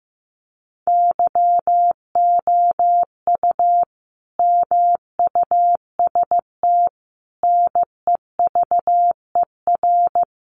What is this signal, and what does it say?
Morse code, heard as a keyed tone, about 15 wpm, 700 Hz. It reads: YOU MUST NEVER
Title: Sentences from Top 300 Words 15wpm